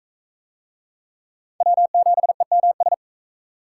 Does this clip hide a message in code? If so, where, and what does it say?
Yes: Morse code from 1.6 to 2.9 s, heard as a keyed tone, about 41 wpm, 700 Hz. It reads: W7EMS